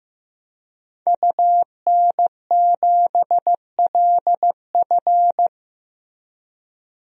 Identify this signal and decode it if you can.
Morse code, heard as a keyed tone, about 15 wpm, 700 Hz. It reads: UN7LF
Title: DXCC-100 Call Signs - 5 Characters - Repeated 3x - 15wpm